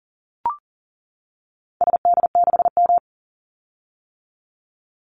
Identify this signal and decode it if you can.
Morse code, heard as a keyed tone, about 38 wpm, 700 Hz. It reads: SD6M
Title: Call Signs - 40wpm